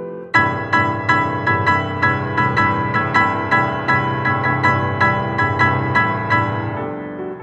There is no code.